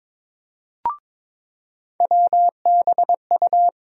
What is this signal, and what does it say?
Morse code, heard as a keyed tone, about 21 wpm, 700 Hz. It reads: WBU
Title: USA Call Sign Postfixes - 3 Letters 22wpm